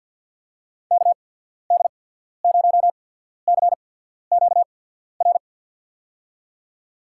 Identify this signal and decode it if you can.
Morse code, heard as a keyed tone, about 53 wpm, spaced as for 14 wpm, 700 Hz. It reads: KD0CQR
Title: USA Call Signs - 6 Characters - 8x Character Spacing - 50wpm